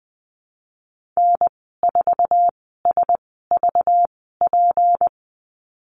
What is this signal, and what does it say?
Morse code, heard as a keyed tone, about 20 wpm, spaced as for 14 wpm, 700 Hz. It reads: N4SVP
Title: 213 Lesson 34g - Number 8 - Call Signs 20wpm